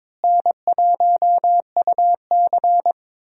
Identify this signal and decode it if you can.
Morse code, heard as a keyed tone, about 22 wpm, 700 Hz. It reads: N1UC
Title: USA Call Signs - 4 Characters - Repeated 2x - 22wpm